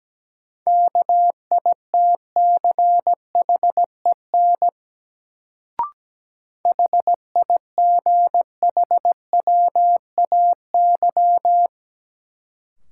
Morse code, heard as a keyed tone, about 17 wpm, 700 Hz. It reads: KITCHEN HIGHWAY
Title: Sentences from Top 1400 Words - New Word Drill Alt - 17wpm